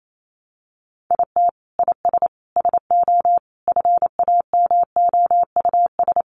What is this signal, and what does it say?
Morse code, heard as a keyed tone, about 28 wpm, 700 Hz. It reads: IT IS SO FAMOUS
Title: Sentences from Top 1500 Words 28wpm